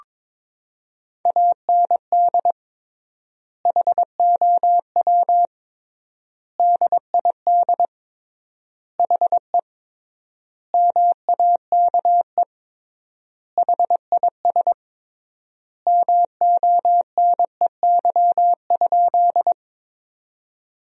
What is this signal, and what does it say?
Morse code, heard as a keyed tone, about 22 wpm, 700 Hz. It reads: AND HOW DID HE MAKE HIS MONEY?
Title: 3x Word-Spacing - Sentences from Top 400 Words - 22wpm